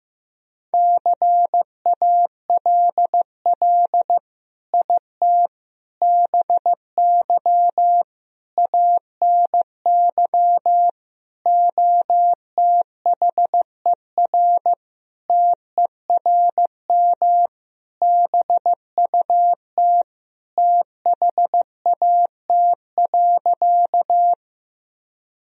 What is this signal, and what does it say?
Morse code, heard as a keyed tone, about 15 wpm, 700 Hz. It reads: CALL IT BY ANY OTHER TERM BUT THAT.
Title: Sentences from Top 1000 Words - Review Part 4 - 15wpm